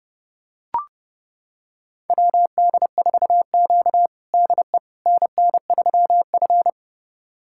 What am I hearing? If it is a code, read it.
Morse code, heard as a keyed tone, about 30 wpm, 700 Hz. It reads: WD4Q DE NN3F
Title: USA Callsign DE Callsign - 4 Characters - 30wpm